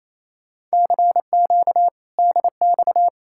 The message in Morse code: CQ DX

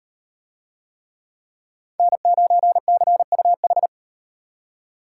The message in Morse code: N9CUH

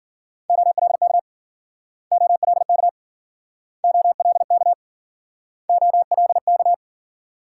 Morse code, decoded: OLK OLK OLK OLK

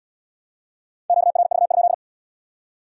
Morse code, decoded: QRV?

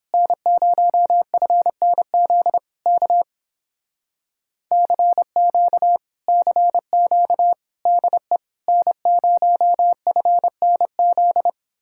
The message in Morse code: N0FNZ K CQ CQ DE N0FNZ

